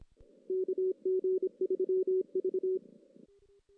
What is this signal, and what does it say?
Morse code, heard as a keyed tone, about 25 wpm, 370 Hz. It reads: KG3V
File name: 2016 JAN CW NAQP / 2016 Jan naqp k5ot - 0-5